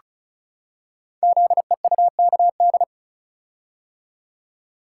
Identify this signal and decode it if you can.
Morse code, heard as a keyed tone, about 36 wpm, 700 Hz. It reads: ZEUKD
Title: Code Groups - 5 Letters - 35wpm